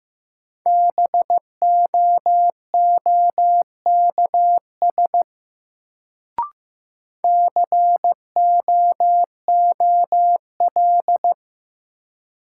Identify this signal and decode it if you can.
Morse code, heard as a keyed tone, about 15 wpm, 700 Hz. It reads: BOOKS COOL
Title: Sentences from Top 1200 Words - New Word Drill Alt - 15wpm